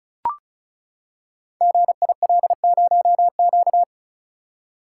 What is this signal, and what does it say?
Morse code, heard as a keyed tone, about 36 wpm, 700 Hz. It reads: GIL0Q
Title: Code Groups - 5 Letter:Numbers - 35wpm